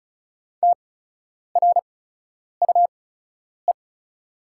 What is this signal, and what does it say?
Morse code, heard as a keyed tone, about 36 wpm, spaced as for 10 wpm, 700 Hz. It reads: TRUE
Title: 8x Character Spacing - Top 500 Words - 35wpm